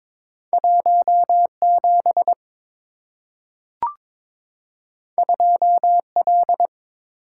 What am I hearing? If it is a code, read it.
Morse code, heard as a keyed tone, about 22 wpm, 700 Hz. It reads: 17 2L
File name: practice / 22wpm / Two Letter-Number Combinations - 22wpm